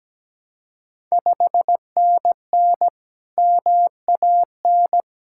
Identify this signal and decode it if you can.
Morse code, heard as a keyed tone, about 17 wpm, 700 Hz. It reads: 5NN MAN
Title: Ontario QSO Party Exchange 17wpm